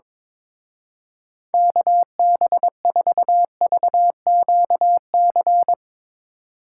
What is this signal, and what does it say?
Morse code, heard as a keyed tone, about 23 wpm, 700 Hz. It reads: KB4VQC
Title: Speed Racing - USA 2x3 Call Signs 22wpm